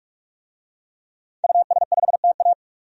Morse code, heard as a keyed tone, about 44 wpm, 700 Hz. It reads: US5TA
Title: DXCC-100 Call Signs - 5 Characters - 45wpm